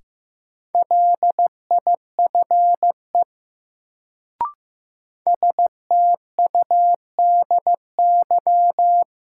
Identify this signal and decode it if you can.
Morse code, heard as a keyed tone, about 15 wpm, 700 Hz. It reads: LIFE STUDY
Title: Sentences from Top 300 Words - New Word Drill - 15wpm